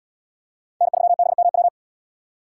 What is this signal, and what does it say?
Morse code, heard as a keyed tone, about 72 wpm, 700 Hz. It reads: N3BMW